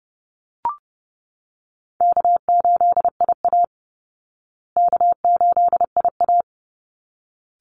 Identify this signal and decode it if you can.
Morse code, heard as a keyed tone, about 30 wpm, 700 Hz. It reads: K8IA K8IA
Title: Call Signs Repeated 3x - 30wpm